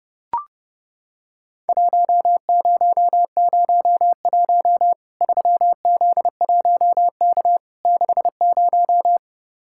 Morse code, heard as a keyed tone, about 30 wpm, 700 Hz. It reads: 1001 3Z1K 60
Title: Worked All Europe DX Contest - QTC Exchange - 30wpm